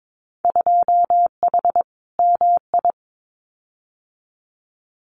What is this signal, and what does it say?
Morse code, heard as a keyed tone, about 22 wpm, 700 Hz. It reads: 2H MI
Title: Winter Field Day Exchanges Encore 22wpm